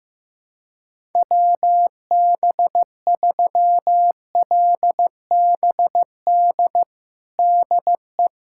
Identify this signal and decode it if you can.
Morse code, heard as a keyed tone, about 15 wpm, 700 Hz. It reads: WB3LBD DE
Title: USA Callsign DE Callsign 15wpm